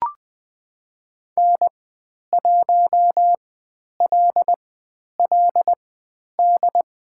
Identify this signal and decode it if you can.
Morse code, heard as a keyed tone, about 20 wpm, spaced as for 10 wpm, 700 Hz. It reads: N1LLD
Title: 095 Lesson 16g - Letter c - Call Signs 20wpm